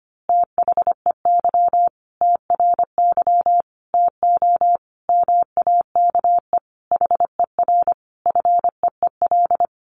Morse code, heard as a keyed tone, about 25 wpm, 700 Hz. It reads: THEY TRY TO MAKE HER FEEL
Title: Easy Phrases 25wpm